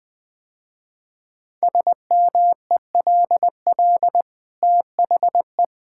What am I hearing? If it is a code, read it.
Morse code, heard as a keyed tone, about 21 wpm, 700 Hz. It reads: SMELL THE
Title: Easy Sentences with Hints - 20wpm